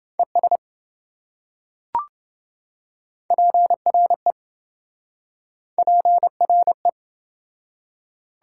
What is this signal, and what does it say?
Morse code, heard as a keyed tone, about 26 wpm, 700 Hz. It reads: ES PRE PRE